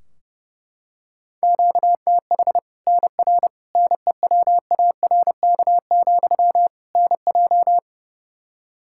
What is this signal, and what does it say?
Morse code, heard as a keyed tone, about 30 wpm, 700 Hz. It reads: QTH NR NEWARK, NJ
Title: Practice QSO - Exchange 1 - Beginning - 30wpm